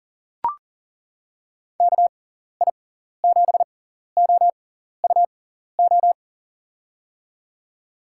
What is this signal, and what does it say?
Morse code, heard as a keyed tone, about 40 wpm, spaced as for 14 wpm, 700 Hz. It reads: KI7OUO
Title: USA Call Signs - 6 Characters - 6x Character Spacing - 40wpm